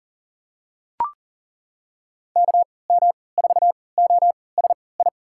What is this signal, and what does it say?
Morse code, heard as a keyed tone, about 40 wpm, spaced as for 23 wpm, 700 Hz. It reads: KM4OSI
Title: USA Call Signs - 6 Characters - 3x Character Spacing - 40wpm